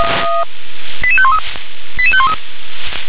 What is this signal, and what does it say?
Five-tone selective call (ZVEI), heard as quick runs of five tones, tones of about 70 ms. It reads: 88513 88513